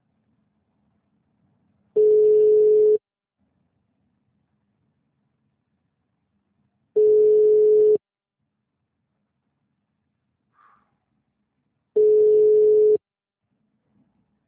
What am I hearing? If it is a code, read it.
Ringback tone (Europe, 425 Hz).
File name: out- -2015- - - .2